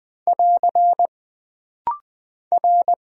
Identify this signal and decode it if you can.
Morse code, heard as a keyed tone, about 20 wpm, 700 Hz. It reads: <AR> R
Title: Single Character (Letter:Number:Punctuation:Prosign) - Rapid-Fire - 20wpm